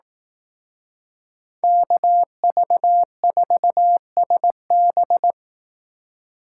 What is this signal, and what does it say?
Morse code, heard as a keyed tone, about 18 wpm, 700 Hz. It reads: KV4SB
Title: Speed Racing - USA 2x2 Call Signs 15wpm